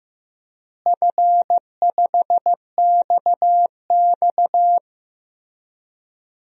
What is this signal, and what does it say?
Morse code, heard as a keyed tone, about 15 wpm, 700 Hz. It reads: F5XX
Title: DXCC-100 Call Signs - 4 Characters - Repeated 3x - 15wpm